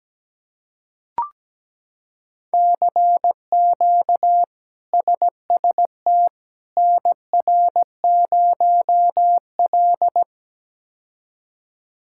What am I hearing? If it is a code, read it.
Morse code, heard as a keyed tone, about 17 wpm, 700 Hz. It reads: CQ SST NR0L
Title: K1USN SST QSO Practice - 17wpm